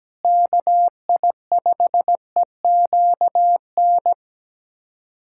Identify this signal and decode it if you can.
Morse code, heard as a keyed tone, about 17 wpm, 700 Hz. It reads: KI5EQN
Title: USA Callsign DE Callsign - 6 Characters - 17wpm